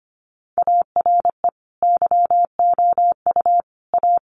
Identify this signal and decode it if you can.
Morse code, heard as a keyed tone, about 25 wpm, 700 Hz. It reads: ARE YOU A
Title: Sentences from Top 1500 Words 25wpm